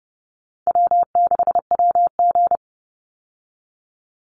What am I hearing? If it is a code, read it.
Morse code, heard as a keyed tone, about 30 wpm, 700 Hz. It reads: W6WG
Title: Call Signs Repeated 3x - 30wpm